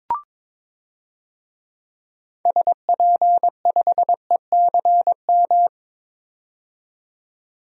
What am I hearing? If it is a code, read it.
Morse code, heard as a keyed tone, about 22 wpm, 700 Hz. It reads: SP5ECM
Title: DXCC-100 Call Signs - 6 Characters - 22wpm